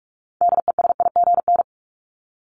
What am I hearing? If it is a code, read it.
Morse code, heard as a keyed tone, about 44 wpm, 700 Hz. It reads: DESIGN
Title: Top 751-1000 Words 45wpm